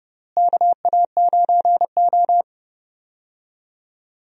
Morse code, heard as a keyed tone, about 31 wpm, 700 Hz. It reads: KA9O
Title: USA Call Signs - 4 Characters - Repeated 2x - 30wpm